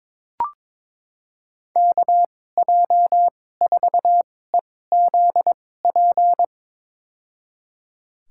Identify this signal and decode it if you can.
Morse code, heard as a keyed tone, about 22 wpm, spaced as for 16 wpm, 700 Hz. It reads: KJ4EZP
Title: USA Call Signs - 6 Characters - 2x Character Spacing - 22wpm